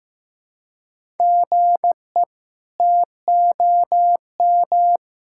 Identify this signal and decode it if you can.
Morse code, heard as a keyed tone, about 15 wpm, 700 Hz. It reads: GE TOM
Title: K1USN SST QSO Practice - 15wpm